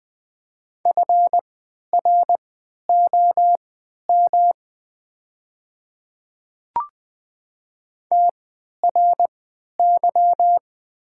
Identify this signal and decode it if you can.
Morse code, heard as a keyed tone, about 20 wpm, spaced as for 11 wpm, 700 Hz. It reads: FROM TRY
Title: 3x Character Spacing - Top 200 Words - 20wpm